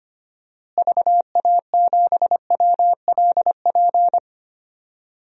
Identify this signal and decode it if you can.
Morse code, heard as a keyed tone, about 25 wpm, 700 Hz. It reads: VA7WLP
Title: DXCC-100 Call Signs - 6 Characters - Repeated 2x - 25wpm